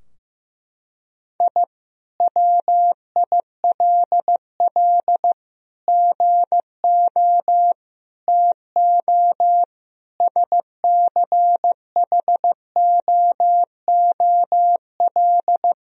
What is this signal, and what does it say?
Morse code, heard as a keyed tone, about 15 wpm, 700 Hz. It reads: I WILL GO TO SCHOOL
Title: Sentences from Top 1000 Words - Review Part 4 - 15wpm